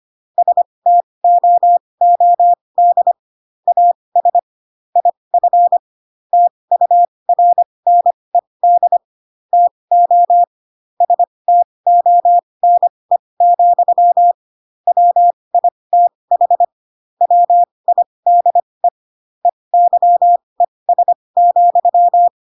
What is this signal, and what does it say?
Morse code, heard as a keyed tone, about 25 wpm, spaced as for 20 wpm, 700 Hz. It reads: STOOD AS IF TURNED TO STONE, WITH WIDE EYES,